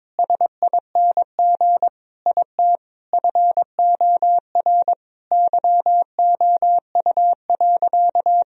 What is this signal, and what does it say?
Morse code, heard as a keyed tone, about 22 wpm, 700 Hz. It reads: SING IT FOR YOU.
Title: Sentences from Top 1000 Words - Review Part 2 - 22wpm